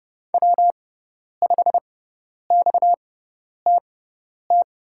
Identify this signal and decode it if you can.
Morse code, heard as a keyed tone, about 30 wpm, spaced as for 10 wpm, 700 Hz. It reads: W5XTT
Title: USA Call Signs - 5 Characters - 6x Character Spacing - 30wpm